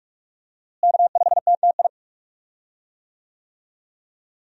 Morse code, heard as a keyed tone, about 42 wpm, 700 Hz. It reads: K5TTI
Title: USA Call Signs - 5 Characters - Repeated 3x - 45wpm